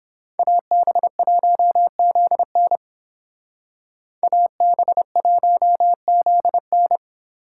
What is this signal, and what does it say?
Morse code, heard as a keyed tone, about 29 wpm, 700 Hz. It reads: AB1ZN AB1ZN